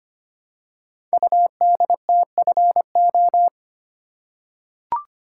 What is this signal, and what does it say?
Morse code, heard as a keyed tone, about 25 wpm, 700 Hz. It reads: UDTFO
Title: Code Groups - 5 Letters - 25wpm